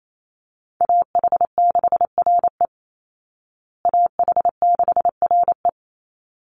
Morse code, heard as a keyed tone, about 28 wpm, 700 Hz. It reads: AH6RE AH6RE
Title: Call Signs Repeated 3x - 28wpm